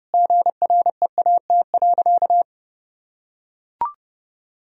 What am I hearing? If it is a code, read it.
Morse code, heard as a keyed tone, about 30 wpm, 700 Hz. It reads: GREAT.
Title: Sentences from Top 1000 Words - Review Part 4 - 30wpm